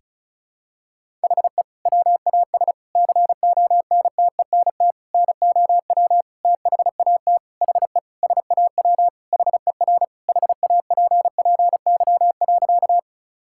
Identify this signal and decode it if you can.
Morse code, heard as a keyed tone, about 34 wpm, 700 Hz. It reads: HE WAS CONTENT NOW THAT HE SAW HER HAPPY.
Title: Sentences from Top 1000 Words - Review Part 6 - 35wpm